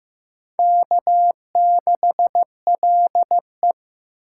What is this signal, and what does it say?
Morse code, heard as a keyed tone, about 15 wpm, 700 Hz. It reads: K6LE
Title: USA Call Signs - 4 Characters - 8x Character Spacing - 15wpm